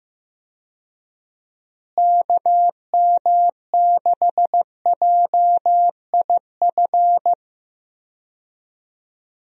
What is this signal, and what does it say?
Morse code, heard as a keyed tone, about 15 wpm, 700 Hz. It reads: KM6JIF
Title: USA Call Signs - 6 Characters - Repeated 2x - 15wpm